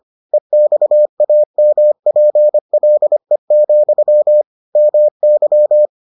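Morse code, heard as a keyed tone, about 25 wpm, 600 Hz. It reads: EXAMPLE, MY
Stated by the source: Morse edition of the Foundations of Amateur Radio episodes.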